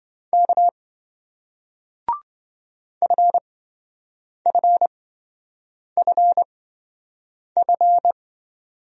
Morse code, heard as a keyed tone, about 28 wpm, spaced as for 7 wpm, 700 Hz. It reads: K FFFF